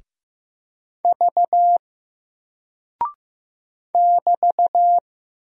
Morse code, heard as a keyed tone, about 15 wpm, 700 Hz. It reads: V <BT>
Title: Single Character (Letter:Number:Punctuation:Prosign) Alt 15wpm